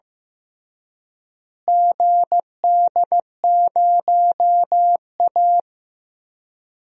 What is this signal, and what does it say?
Morse code, heard as a keyed tone, about 15 wpm, 700 Hz. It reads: GD0A
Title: DXCC-100 Call Signs - 4 Characters - Repeated 2x - 15wpm